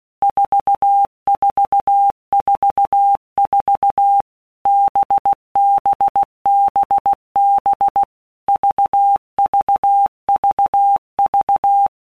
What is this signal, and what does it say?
Morse code, heard as a keyed tone, about 16 wpm, 800 Hz. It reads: 4444 BBBB VVVV
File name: Basic Letter Groups By WPM's / Groups@15 / VB46@15